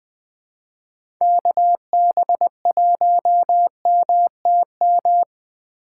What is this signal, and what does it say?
Morse code, heard as a keyed tone, about 20 wpm, 700 Hz. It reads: KB1MTM
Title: USA Call Signs - 6 Characters - 3x Character Spacing - 20wpm